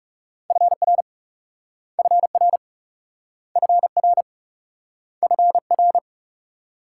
Morse code, heard as a keyed tone, about 41 wpm, 700 Hz. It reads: FR FR FR FR